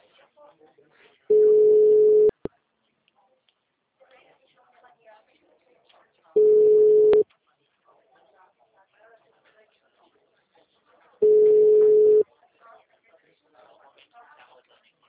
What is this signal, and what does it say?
Ringback tone (Europe, 425 Hz), with speech.